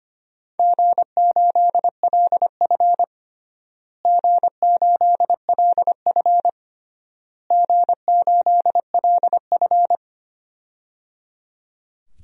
Morse code, heard as a keyed tone, about 25 wpm, 700 Hz. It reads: G8LF G8LF G8LF
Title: DXCC-100 Call Signs - 4 Characters - Repeated 3x - 25wpm